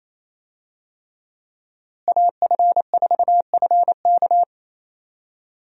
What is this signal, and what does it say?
Morse code, heard as a keyed tone, about 28 wpm, 700 Hz. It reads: AF4FK
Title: USA Call Signs - 5 Characters - Repeated 3x - 28wpm